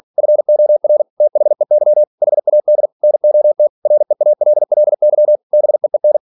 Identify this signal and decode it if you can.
Morse code, heard as a keyed tone, about 46 wpm, 600 Hz. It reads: FOR THEY HAD NOT REALLY BEEN
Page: Never 47wpm